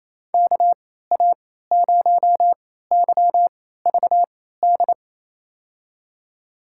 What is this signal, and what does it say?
Morse code, heard as a keyed tone, about 28 wpm, spaced as for 16 wpm, 700 Hz. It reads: KA0YVD
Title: USA Call Signs - 6 Characters - 3x Character Spacing - 28wpm